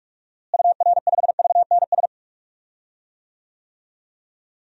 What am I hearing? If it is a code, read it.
Morse code, heard as a keyed tone, about 42 wpm, 700 Hz. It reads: UR5VNS
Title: DXCC-100 Call Signs - 6 Characters - Repeated 2x - 45wpm